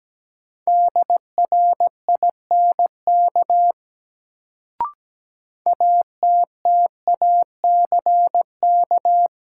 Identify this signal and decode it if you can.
Morse code, heard as a keyed tone, about 17 wpm, 700 Hz. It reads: DRINK ATTACK
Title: Sentences from Top 1500 Words - New Word Drill - 17wpm